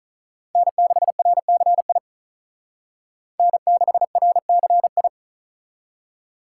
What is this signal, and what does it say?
Morse code, heard as a keyed tone, about 42 wpm, 700 Hz. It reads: N6RCI N6RCI
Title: Speed Racing - USA 1x3 Call Signs 35wpm